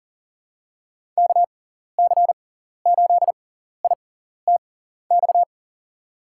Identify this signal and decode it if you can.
Morse code, heard as a keyed tone, about 38 wpm, spaced as for 14 wpm, 700 Hz. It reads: KC8ITX
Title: USA Call Signs - 6x Character Spacing - 40wpm